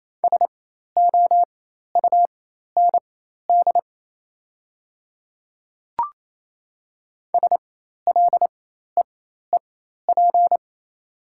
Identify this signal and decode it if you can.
Morse code, heard as a keyed tone, about 28 wpm, spaced as for 13 wpm, 700 Hz. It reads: SOUND SLEEP
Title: 4x Character Spacing - Top 500 Words - 28wpm